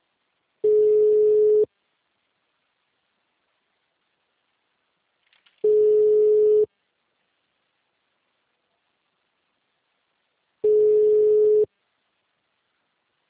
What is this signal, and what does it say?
Ringback tone (Europe, 425 Hz).
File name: out- -2002- - - .12